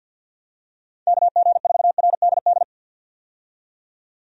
Morse code, heard as a keyed tone, about 52 wpm, 700 Hz. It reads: KG4RDD
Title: USA Call Signs - 6 Characters - 50wpm